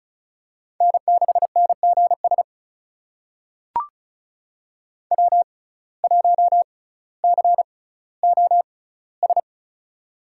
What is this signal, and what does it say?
Morse code, heard as a keyed tone, about 34 wpm, spaced as for 18 wpm, 700 Hz. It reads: N6NGS W1COS
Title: USA Call Signs - 5 Characters - 6x Character Spacing - 35wpm